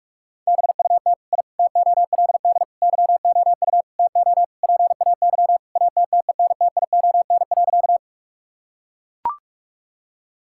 Morse code, heard as a keyed tone, about 42 wpm, 700 Hz. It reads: BUT I TOLD YOU TO PAY ATTENTION.
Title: Sentences from Top 1000 Words - Review Part 6 - 45wpm